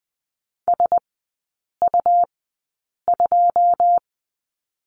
Morse code, heard as a keyed tone, about 20 wpm, spaced as for 8 wpm, 700 Hz. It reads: SU2
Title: 100 Lesson 17e - Letter u - Three Characters 20wpm